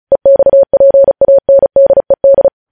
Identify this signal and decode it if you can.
Morse code, heard as a keyed tone, about 36 wpm, 550 Hz. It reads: EXPANDED
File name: podcast / News headlines-1x-35x35